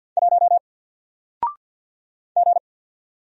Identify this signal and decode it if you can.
Morse code, heard as a keyed tone, about 48 wpm, 700 Hz. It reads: 1 G
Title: Single Character (Letter:Number:Punctuation:Prosign) Alt 50wpm